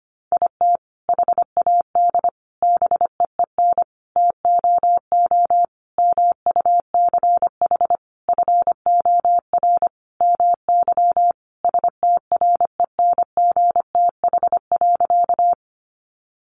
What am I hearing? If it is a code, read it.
Morse code, heard as a keyed tone, about 25 wpm, 700 Hz. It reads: IT HAD BEEN TOO MUCH FOR MY STRENGTH.
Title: Sentences from Top 1300 Words 25wpm